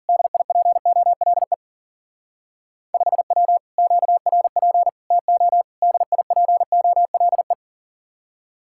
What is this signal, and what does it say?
Morse code, heard as a keyed tone, about 47 wpm, 700 Hz. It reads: DIPOLE 5W QRP TO DIPOLE
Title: Speed-Racing Common QSO Phrases Encore - 40wpm